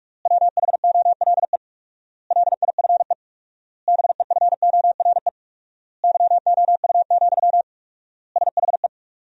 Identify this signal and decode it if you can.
Morse code, heard as a keyed tone, about 46 wpm, 700 Hz. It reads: WHOLE LIFE BEFORE YOU, SHE
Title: 4x Word-Spacing - Sentences from Top 400 Words - 45wpm